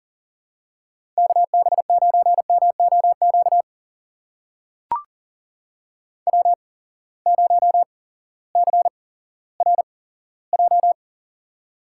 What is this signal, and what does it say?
Morse code, heard as a keyed tone, about 39 wpm, spaced as for 18 wpm, 700 Hz. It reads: KB9MOQ W0CRJ
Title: USA Call Signs - 8x Character Spacing - 40wpm